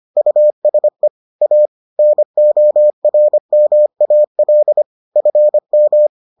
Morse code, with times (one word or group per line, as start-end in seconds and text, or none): 0.2-1.1 USE
1.4-1.7 A
2.0-4.8 NORMAL
5.2-6.1 FM